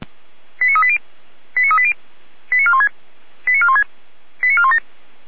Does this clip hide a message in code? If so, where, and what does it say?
Yes: five-tone selective call (ZVEI) from 0.6 to 4.8 s, heard as quick runs of five tones, tones of about 70 ms. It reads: 89380 89380 89416 89416 89417